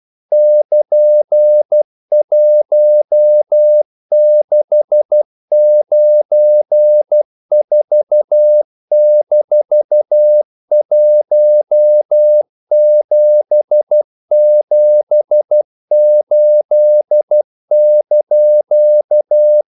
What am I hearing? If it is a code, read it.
Morse code, heard as a keyed tone, about 12 wpm, 600 Hz. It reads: <KN>1694-1778)
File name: CW / Practice / EnglishQuotations0312